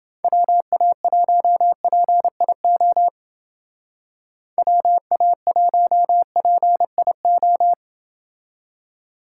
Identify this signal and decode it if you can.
Morse code, heard as a keyed tone, about 29 wpm, 700 Hz. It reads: WA1PIO WA1PIO